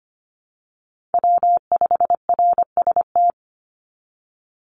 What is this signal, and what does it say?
Morse code, heard as a keyed tone, about 25 wpm, 700 Hz. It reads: W5RST
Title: NA SOTA Chaser Callsigns 25wpm